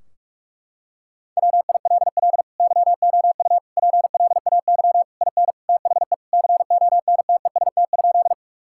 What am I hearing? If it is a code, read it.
Morse code, heard as a keyed tone, about 42 wpm, 700 Hz. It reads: WILL YOU PLAY IN THE CONTEST?
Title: Sentences from Top 1600 Words 45wpm